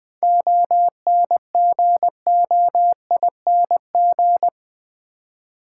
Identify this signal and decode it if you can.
Morse code, heard as a keyed tone, about 20 wpm, 700 Hz. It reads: ONGOING
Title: 7 Letter Words - First 3 Letters Given - 20wpm